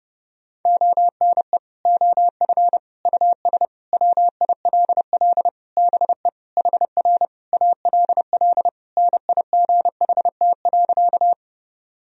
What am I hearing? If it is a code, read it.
Morse code, heard as a keyed tone, about 30 wpm, 700 Hz. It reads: ONE OF US WILL BE HR ALL NIGHT.
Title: Sentences from Top 300 Words and Abbreviations 30wpm